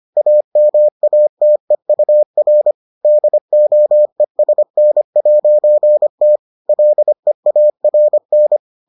Morse code, heard as a keyed tone, about 25 wpm, 600 Hz. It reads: AMATEUR DOESN'T LEARN